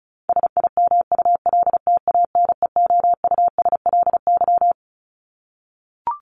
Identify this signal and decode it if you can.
Morse code, heard as a keyed tone, about 34 wpm, 700 Hz. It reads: SIMULTANEOUSLY
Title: LY Suffix - Top 100 Words with Base Words 35wpm